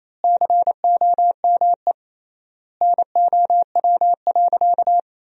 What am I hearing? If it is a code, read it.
Morse code, heard as a keyed tone, about 28 wpm, 700 Hz. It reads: COME NOW.
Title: 3x Word-Spacing - Sentences from Top 200 Words - 28wpm